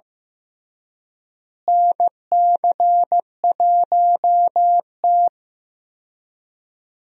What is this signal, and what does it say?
Morse code, heard as a keyed tone, about 15 wpm, 700 Hz. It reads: NC1T
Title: USA Call Signs - 4 Characters - Repeated 3x - 15wpm